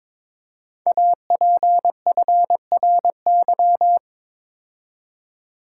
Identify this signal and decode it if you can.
Morse code, heard as a keyed tone, about 22 wpm, 700 Hz. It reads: APFRY
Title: Code Groups - 5 Letters - 22wpm